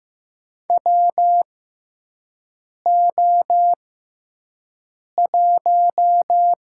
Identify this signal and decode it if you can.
Morse code, heard as a keyed tone, about 15 wpm, spaced as for 5 wpm, 700 Hz. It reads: WO1
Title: USA Call Signs - 6x Character Spacing - 15wpm